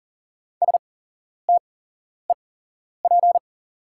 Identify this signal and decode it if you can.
Morse code, heard as a keyed tone, about 39 wpm, spaced as for 11 wpm, 690 Hz. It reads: STEP
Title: 8x Character Spacing - Top 500 Words - 40wpm